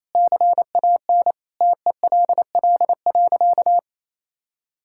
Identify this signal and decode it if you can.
Morse code, heard as a keyed tone, about 28 wpm, 700 Hz. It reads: CAN TELL.
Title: Sentences from Top 1600 Words 28wpm